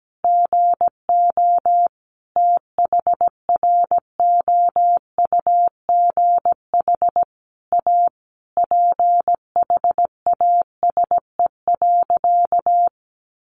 Morse code, heard as a keyed tone, about 17 wpm, 700 Hz. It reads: GO THROUGH A PHASE.
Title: Sentences from Top 1200 Words 17wpm